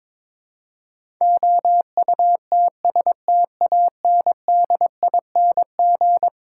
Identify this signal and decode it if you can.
Morse code, heard as a keyed tone, about 22 wpm, 700 Hz. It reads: OUTSTANDING
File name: practice / 22wpm / Eleven Letter Words - 22wpm